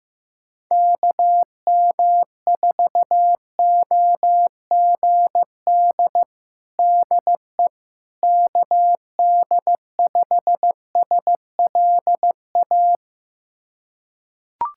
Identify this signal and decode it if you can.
Morse code, heard as a keyed tone, about 15 wpm, 700 Hz. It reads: KM4OGD DE KD5SLA
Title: USA Callsign DE Callsign - 6 Characters - 15wpm